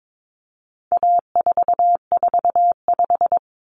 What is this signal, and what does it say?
Morse code, heard as a keyed tone, about 22 wpm, 700 Hz. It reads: A445
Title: Common Cut Numbers - 4-Digit 22wpm